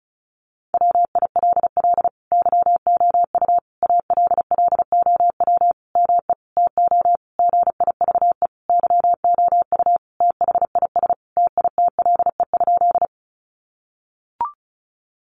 Morse code, heard as a keyed tone, about 34 wpm, 700 Hz. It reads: WILL YOU ALLOW ME TO GIVE YOU THIS TITLE?
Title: Sentences from Top 1200 Words 35wpm